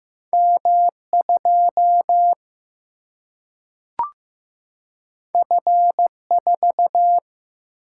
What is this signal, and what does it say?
Morse code, heard as a keyed tone, about 15 wpm, 700 Hz. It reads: M 2 F 4
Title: Single-Letter Single-Number 15wpm